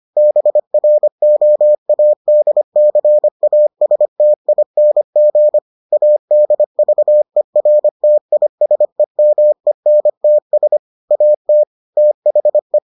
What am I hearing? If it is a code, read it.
Morse code, heard as a keyed tone, about 25 wpm, 600 Hz. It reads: BROADCASTING ADVERTISEMENTS AT THE